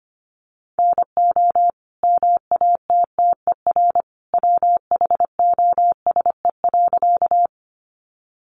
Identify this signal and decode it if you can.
Morse code, heard as a keyed tone, about 25 wpm, 700 Hz. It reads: NO MATTER WHOSE.
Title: Sentences from Top 1200 Words 25wpm